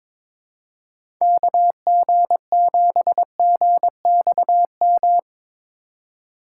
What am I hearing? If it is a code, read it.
Morse code, heard as a keyed tone, about 22 wpm, 700 Hz. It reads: KG7GXM
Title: USA Call Signs - 6 Characters - 3x Character Spacing - 22wpm